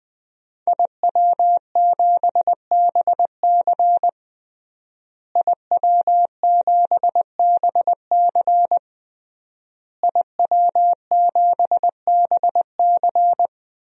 Morse code, heard as a keyed tone, about 20 wpm, 700 Hz. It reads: IW7BC IW7BC IW7BC
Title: DXCC-100 Call Signs - 5 Characters - Repeated 3x - 20wpm